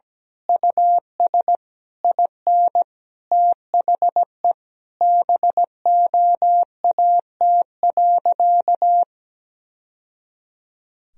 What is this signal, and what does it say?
Morse code, heard as a keyed tone, about 17 wpm, 700 Hz. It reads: US IN THE BOAT.
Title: Sentences from Top 1000 Words - Review Part 3 - 17wpm